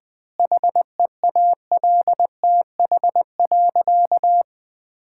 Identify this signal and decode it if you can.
Morse code, heard as a keyed tone, about 20 wpm, 700 Hz. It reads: HEALTH.
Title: Difficult Sentences with Hints - 20wpm